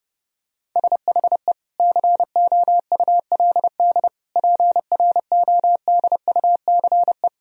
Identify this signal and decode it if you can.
Morse code, heard as a keyed tone, about 30 wpm, 700 Hz. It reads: SHE COULD PRODUCE
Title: Sentences from Top 1000 Words - Review Part 3 - 30wpm